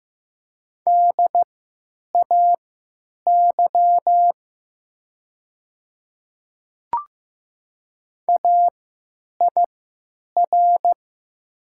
Morse code, heard as a keyed tone, about 15 wpm, spaced as for 9 wpm, 700 Hz. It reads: DAY AIR
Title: 3x Character Spacing - Top 200 Words - 15wpm